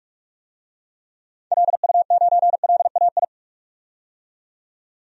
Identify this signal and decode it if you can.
Morse code, heard as a keyed tone, about 46 wpm, 700 Hz. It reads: LU9LAI
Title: DXCC-100 Call Signs - 6 Characters - 45wpm